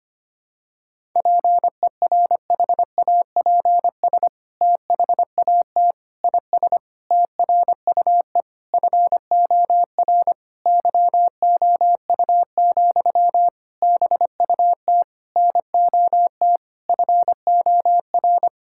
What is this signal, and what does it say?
Morse code, heard as a keyed tone, about 25 wpm, 700 Hz. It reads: PERHAPS THAT IS TRUE FOR YOU, BUT NOT FOR
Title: Sentences from Top 1000 Words - Review Part 5 - 25wpm